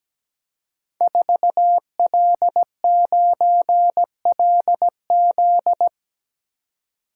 Morse code, heard as a keyed tone, about 17 wpm, 700 Hz. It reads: 4L9LZ
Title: Code Groups - 5 Letter:Numbers - 17wpm